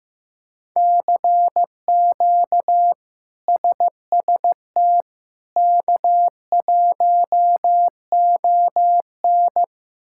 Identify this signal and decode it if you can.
Morse code, heard as a keyed tone, about 15 wpm, 700 Hz. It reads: CQ SST K1ON
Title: K1USN SST QSO Practice - 15wpm